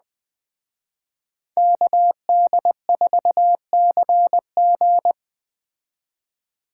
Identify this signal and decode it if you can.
Morse code, heard as a keyed tone, about 20 wpm, 700 Hz. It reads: KD4CG